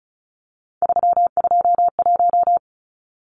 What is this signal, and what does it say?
Morse code, heard as a keyed tone, about 34 wpm, 700 Hz. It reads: 321
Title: Numbers - 3-Digit 35wpm